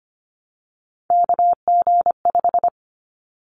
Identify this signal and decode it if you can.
Morse code, heard as a keyed tone, about 25 wpm, 700 Hz. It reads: KG5
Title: Callsign Prefixes for Top 25 DXCC - 25wpm